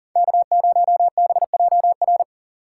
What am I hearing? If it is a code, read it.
Morse code, heard as a keyed tone, about 40 wpm, 700 Hz. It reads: K0BJR
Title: USA Call Signs - 6x Character Spacing - 40wpm